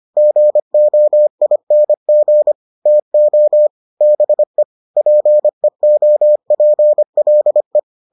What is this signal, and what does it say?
Morse code, heard as a keyed tone, about 25 wpm, 600 Hz. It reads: GOING TO BE PEOPLE